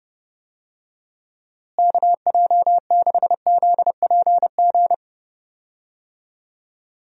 Morse code, heard as a keyed tone, about 30 wpm, 700 Hz. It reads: KJ6ZPG
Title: USA Call Signs - 6 Characters - Repeated 2x - 30wpm